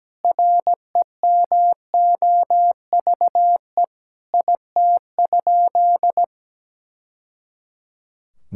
Morse code, heard as a keyed tone, about 17 wpm, 700 Hz. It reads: REMOVE IT?
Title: Sentences from Top 1000 Words - Review Part 5 - 17wpm